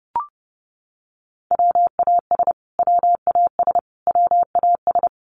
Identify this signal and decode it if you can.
Morse code, heard as a keyed tone, about 30 wpm, 700 Hz. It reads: WAS WAS WAS
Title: Top 200 Words Repeated 30wpm